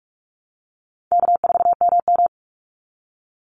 Morse code, heard as a keyed tone, about 44 wpm, 700 Hz. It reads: K4MM
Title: Call Signs Repeated 3x - 45wpm